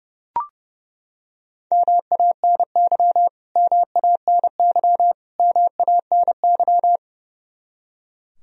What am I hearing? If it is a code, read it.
Morse code, heard as a keyed tone, about 32 wpm, 700 Hz. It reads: MANY MANY MANY
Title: Top 100 Words Repeated 30wpm